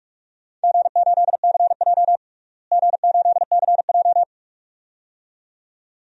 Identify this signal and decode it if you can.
Morse code, heard as a keyed tone, about 41 wpm, 700 Hz. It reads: G8CJ G8CJ
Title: DXCC-100 Call Signs - 4 Characters - Repeated 2x - 45wpm